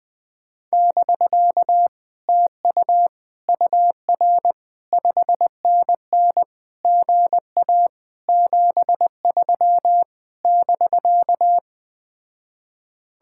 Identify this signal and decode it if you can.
Morse code, heard as a keyed tone, about 20 wpm, 700 Hz. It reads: <BK> TU UR 5NN GA 73 <BK>
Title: Speed-Racing Common QSO Phrases - 17wpm